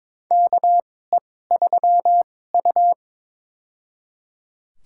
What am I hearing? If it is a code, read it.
Morse code, heard as a keyed tone, about 22 wpm, spaced as for 16 wpm, 700 Hz. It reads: KE3U
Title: USA Call Signs - 4 Characters - 2x Character Spacing - 22wpm